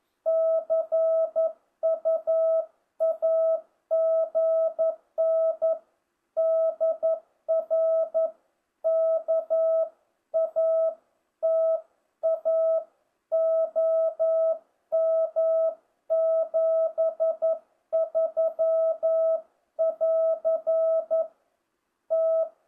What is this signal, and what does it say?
Morse code, heard as a keyed tone, about 11 wpm, 650 Hz. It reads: CUAGN DR KATA OM 73 <AR> T